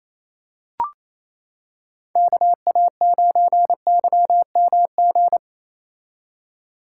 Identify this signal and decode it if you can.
Morse code, heard as a keyed tone, about 28 wpm, 700 Hz. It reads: KA9YMG
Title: K1USN SST QSO Practice - 28wpm